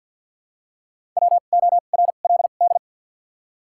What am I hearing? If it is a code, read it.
Morse code, heard as a keyed tone, about 49 wpm, spaced as for 36 wpm, 700 Hz. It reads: WORLD
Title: 2x Character Spacing - Top 200 Words - 50wpm